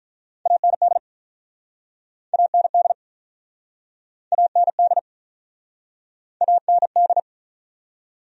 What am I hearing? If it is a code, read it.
Morse code, heard as a keyed tone, about 49 wpm, 700 Hz. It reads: AND AND AND AND